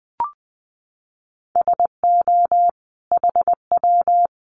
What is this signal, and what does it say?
Morse code, heard as a keyed tone, about 20 wpm, 700 Hz. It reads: SO HW
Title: Common QSO Phrases 20wpm